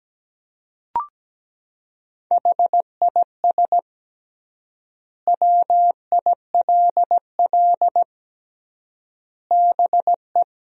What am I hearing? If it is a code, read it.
Morse code, heard as a keyed tone, about 17 wpm, 700 Hz. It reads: HIS WILL BE
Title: 3x Word-Spacing - Sentences from Top 500 Words - 17wpm